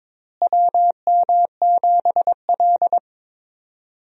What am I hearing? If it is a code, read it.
Morse code, heard as a keyed tone, about 22 wpm, 700 Hz. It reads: WM7L
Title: USA Call Signs - 4 Characters - Repeated 3x - 22wpm